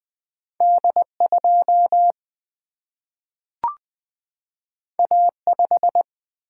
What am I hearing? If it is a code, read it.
Morse code, heard as a keyed tone, about 20 wpm, 700 Hz. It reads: D2 A5
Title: Single-Letter Single-Number 20wpm